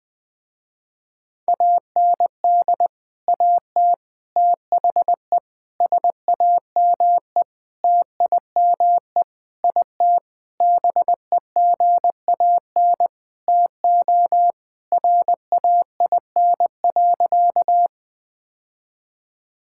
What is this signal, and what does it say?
Morse code, heard as a keyed tone, about 20 wpm, 700 Hz. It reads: AND AT THE SAME TIME IT BEGAN TO RAIN.
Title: Sentences with 5-Word Phrases - Spoken Beginning - 20wpm